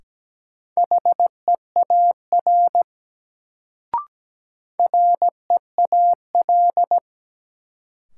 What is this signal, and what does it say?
Morse code, heard as a keyed tone, about 17 wpm, 700 Hz. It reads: HEAR REAL
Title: Sentences from Top 300 Words - New Word Drill - 17wpm